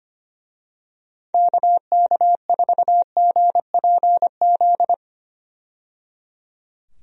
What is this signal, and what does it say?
Morse code, heard as a keyed tone, about 25 wpm, 700 Hz. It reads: KK4GPZ
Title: USA Call Signs - 6 Characters - 25wpm